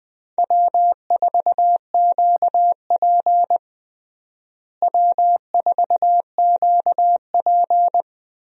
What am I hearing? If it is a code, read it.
Morse code, heard as a keyed tone, about 20 wpm, 700 Hz. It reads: W4QP W4QP
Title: USA Call Signs - 4 Characters - Repeated 3x - 20wpm